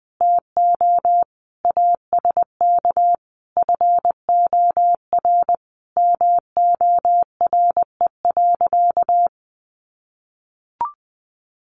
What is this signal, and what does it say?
Morse code, heard as a keyed tone, about 20 wpm, 700 Hz. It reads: TO ASK FOR MORE.
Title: Sentences from Top 1100 Words 20wpm